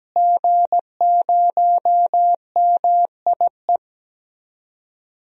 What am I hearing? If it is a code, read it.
Morse code, heard as a keyed tone, about 17 wpm, 700 Hz. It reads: G0MIE
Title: DXCC-100 Call Signs - 5 Characters - Repeated 3x - 17wpm